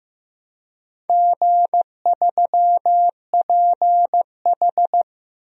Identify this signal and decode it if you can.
Morse code, heard as a keyed tone, about 15 wpm, 700 Hz. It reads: G3PH
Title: DXCC-100 Call Signs - 4 Characters - 15wpm